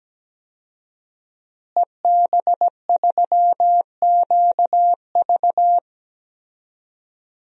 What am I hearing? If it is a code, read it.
Morse code, heard as a keyed tone, about 17 wpm, 700 Hz. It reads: EB3QV
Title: DXCC-100 Call Signs - 5 Characters - Repeated 3x - 17wpm